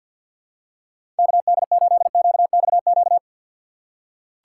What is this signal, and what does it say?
Morse code, heard as a keyed tone, about 47 wpm, 700 Hz. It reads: KD8QXQ
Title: USA Call Signs - 4x Character Spacing - 50wpm